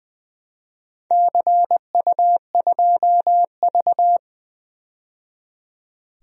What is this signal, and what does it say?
Morse code, heard as a keyed tone, about 20 wpm, 700 Hz. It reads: CU2V
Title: DXCC-100 Call Signs - 4 Characters - 20wpm